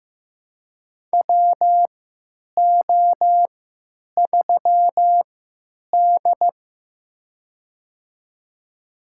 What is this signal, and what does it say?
Morse code, heard as a keyed tone, about 15 wpm, spaced as for 9 wpm, 700 Hz. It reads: WO3D
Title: USA Call Signs - 3x Character Spacing - 15wpm